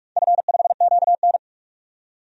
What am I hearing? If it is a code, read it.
Morse code, heard as a keyed tone, about 46 wpm, 700 Hz. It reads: F5QN
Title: DXCC-100 Call Signs - 4 Characters - 45wpm